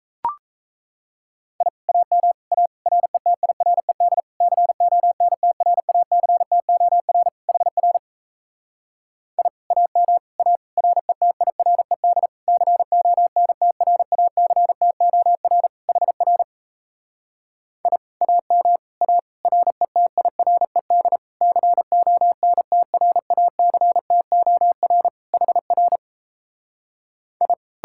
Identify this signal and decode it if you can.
Morse code, heard as a keyed tone, about 37 wpm, 700 Hz. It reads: I AM A RETIRED CONTRACTOR HR I AM A RETIRED CONTRACTOR HR I AM A RETIRED CONTRACTOR HR I